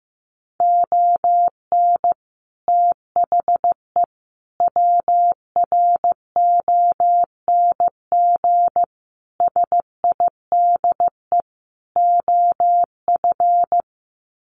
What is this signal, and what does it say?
Morse code, heard as a keyed tone, about 15 wpm, 700 Hz. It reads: ON THE WRONG SIDE OF
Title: Sets of 5 Words Encore 15wpm